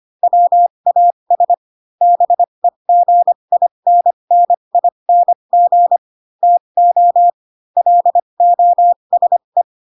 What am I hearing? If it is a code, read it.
Morse code, heard as a keyed tone, about 25 wpm, spaced as for 22 wpm, 700 Hz. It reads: WAS BEGINNING TO LOSE
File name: Pinocchio / Pinocchio ch4 18 25 ik1hge